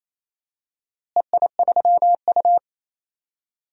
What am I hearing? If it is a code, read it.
Morse code, heard as a keyed tone, about 28 wpm, 700 Hz. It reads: EI3U
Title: DXCC-100 Call Signs - 4 Characters - Repeated 3x - 28wpm